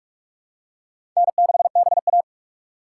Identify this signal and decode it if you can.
Morse code, heard as a keyed tone, about 46 wpm, 700 Hz. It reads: N6BA
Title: USA Call Signs - 4 Characters - 2x Character Spacing - 45wpm